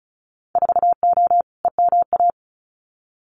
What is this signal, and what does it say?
Morse code, heard as a keyed tone, about 34 wpm, 700 Hz. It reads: 4O EMA
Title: Winter Field Day Exchanges 35wpm